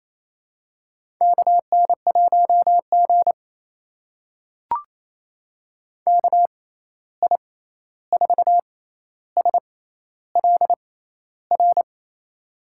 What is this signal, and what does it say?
Morse code, heard as a keyed tone, about 28 wpm, spaced as for 13 wpm, 700 Hz. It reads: KN1G KI4SLR
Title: USA Call Signs - 6x Character Spacing - 28wpm